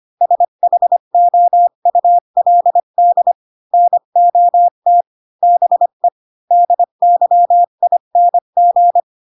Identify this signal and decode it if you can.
Morse code, heard as a keyed tone, about 25 wpm, 700 Hz. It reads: SHOULD NOT BE DYING
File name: Pinocchio / Pinocchio ch5 20 25 ik1hge